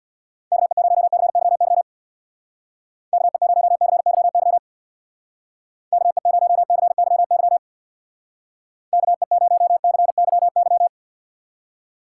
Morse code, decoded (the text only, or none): KE0CYY KE0CYY KE0CYY KE0CYY